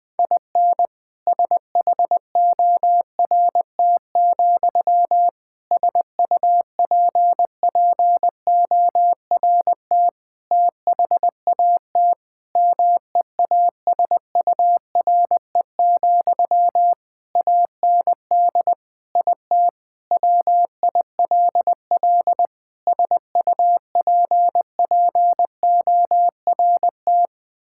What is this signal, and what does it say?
Morse code, heard as a keyed tone, about 20 wpm, 700 Hz. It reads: IN SHORT, SUPPORT THAT MEASURE, AND IT WILL SUPPORT
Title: Sentences from Top 1000 Words - Review Part 5 - 20wpm